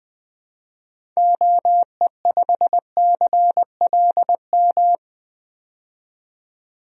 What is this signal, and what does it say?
Morse code, heard as a keyed tone, about 20 wpm, 700 Hz. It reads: OE5CLM
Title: DXCC-100 Call Signs - 6 Characters - 20wpm